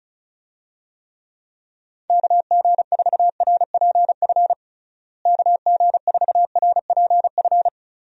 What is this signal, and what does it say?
Morse code, heard as a keyed tone, about 34 wpm, 700 Hz. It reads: KG4RPF KG4RPF
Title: USA Call Signs - 6 Characters - Repeated 3x - 35wpm